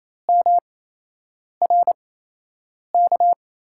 Morse code, heard as a keyed tone, about 28 wpm, spaced as for 8 wpm, 700 Hz. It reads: MRK
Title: USA Call Signs - 8x Character Spacing - 28wpm